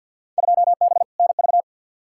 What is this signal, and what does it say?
Morse code, heard as a keyed tone, about 52 wpm, 700 Hz. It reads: 2B NV
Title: Field Day Exchanges Encore - 50wpm